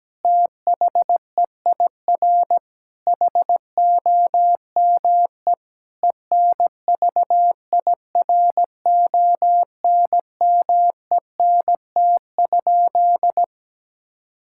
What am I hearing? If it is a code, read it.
Morse code, heard as a keyed tone, about 17 wpm, 700 Hz. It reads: THEIR HOME ENVIRONMENT?
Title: Sentences from Top 1000 Words - Review Part 5 - 17wpm